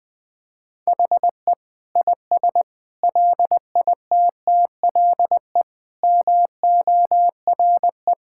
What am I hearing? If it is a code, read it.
Morse code, heard as a keyed tone, about 21 wpm, 700 Hz. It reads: HE IS LITTLE MORE
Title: Sentences from Top 1600 Words 20wpm